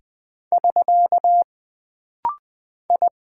Morse code, heard as a keyed tone, about 20 wpm, 700 Hz. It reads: <SK> I
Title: Single Character (Letter:Number:Punctuation:Prosign) - Rapid-Fire - 20wpm